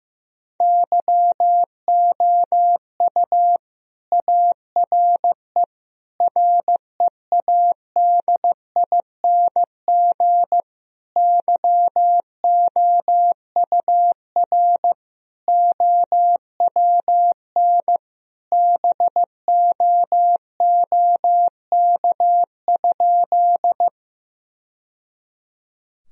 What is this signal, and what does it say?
Morse code, heard as a keyed tone, about 15 wpm, 700 Hz. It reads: YOU ARE READING YOUR OWN BOOK?
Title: Sentences from Top 1000 Words - Review Part 3 - 15wpm